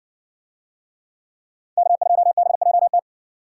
Morse code, heard as a keyed tone, about 55 wpm, 700 Hz. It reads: K2BJT